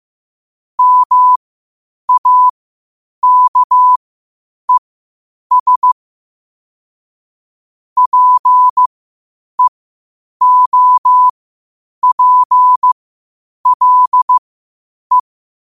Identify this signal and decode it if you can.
Morse code, heard as a keyed tone, about 15 wpm, spaced as for 8 wpm, 1000 Hz. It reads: MAKES PEOPLE
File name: qotdfiles / 8wpm